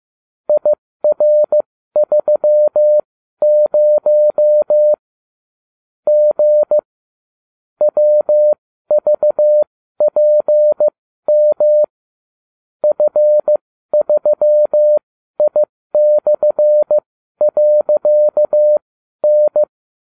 Morse code, heard as a keyed tone, about 15 wpm, spaced as for 12 wpm, 600 Hz. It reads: IR30 G WVPM F3I/.N